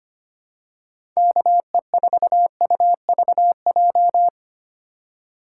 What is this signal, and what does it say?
Morse code, heard as a keyed tone, about 25 wpm, 700 Hz. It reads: KE4UVJ
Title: USA Call Signs - 6 Characters - 4x Character Spacing - 25wpm